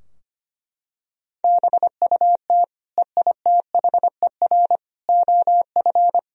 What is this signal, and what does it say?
Morse code, heard as a keyed tone, about 25 wpm, 700 Hz. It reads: BUT EITHER OF
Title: Sentences from Top 1000 Words - Review Part 4 - 25wpm